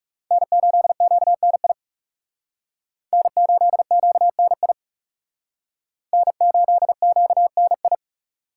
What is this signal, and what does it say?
Morse code, heard as a keyed tone, about 36 wpm, 700 Hz. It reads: N8QNI N8QNI N8QNI
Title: Speed Racing - USA 1x3 Call Signs 30wpm